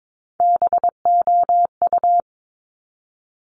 Morse code, heard as a keyed tone, about 22 wpm, 700 Hz. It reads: BOU
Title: Top Three Letter Combinations in Words 22wpm